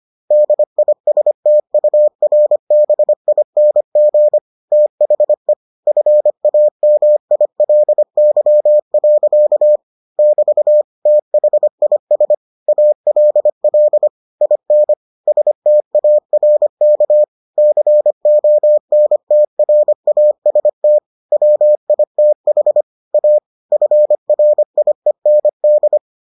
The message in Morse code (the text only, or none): DISTURBING THE FAMILY. <BT> THIS ALL IN STARK CONTRAST WITH A FRIEND